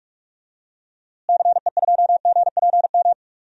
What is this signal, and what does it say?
Morse code, heard as a keyed tone, about 41 wpm, 700 Hz. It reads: CE2GPM
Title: DXCC-100 Call Signs - 6 Characters - Repeated 3x - 45wpm